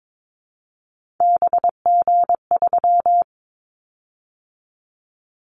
Morse code, heard as a keyed tone, about 22 wpm, 700 Hz. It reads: BG3
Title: Callsign Prefixes for Top 100 DXCC - 22wpm